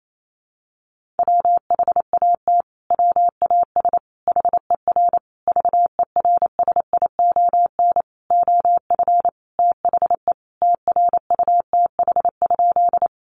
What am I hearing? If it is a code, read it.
Morse code, heard as a keyed tone, about 28 wpm, 700 Hz. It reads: WHAT WAS HER VERSION OF THE TRUTH?
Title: Sentences from Top 1400 Words 28wpm